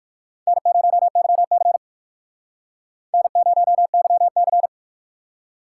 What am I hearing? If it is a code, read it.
Morse code, heard as a keyed tone, about 59 wpm, 700 Hz. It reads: N0YC N0YC